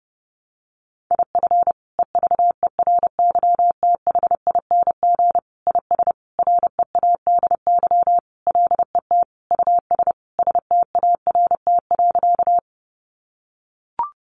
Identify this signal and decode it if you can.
Morse code, heard as a keyed tone, about 30 wpm, 700 Hz. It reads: IF EVERYTHING IS READY LET US START.
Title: Sentences from Top 700 Words 30wpm